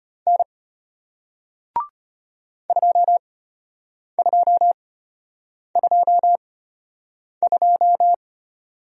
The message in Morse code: N 2222

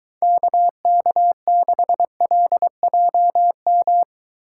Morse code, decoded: KK6LJM

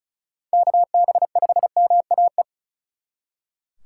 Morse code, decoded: KB5MAE

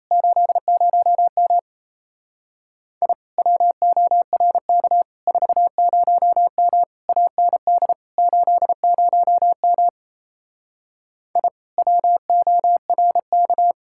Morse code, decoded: 80M I WORK 40M AND 80M I WORK

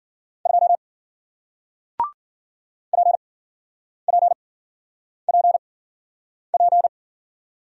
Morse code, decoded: 2 P P P P

